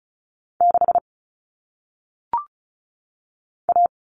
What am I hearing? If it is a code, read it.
Morse code, heard as a keyed tone, about 36 wpm, 700 Hz. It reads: 6 A
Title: All Cut Numbers - 1-Digit 35wpm